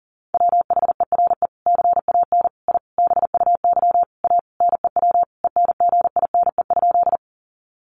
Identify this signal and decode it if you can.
Morse code, heard as a keyed tone, about 37 wpm, 700 Hz. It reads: WHERE CAN I BUY A NEW ENGINE?
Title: Sentences from Top 1500 Words 40wpm